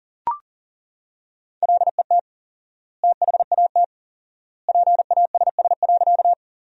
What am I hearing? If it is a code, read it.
Morse code, heard as a keyed tone, about 40 wpm, 700 Hz. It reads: LET THAT PASS.
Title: 4x Word-Spacing - Sentences from Top 400 Words - 40wpm